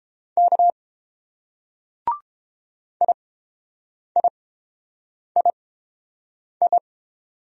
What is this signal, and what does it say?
Morse code, heard as a keyed tone, about 28 wpm, spaced as for 7 wpm, 700 Hz. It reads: K IIII